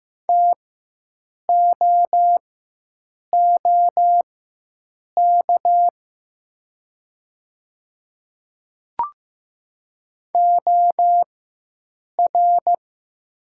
Morse code, heard as a keyed tone, about 15 wpm, spaced as for 7 wpm, 700 Hz. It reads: TOOK OR
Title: 4x Character Spacing - Top 300 Words - 15wpm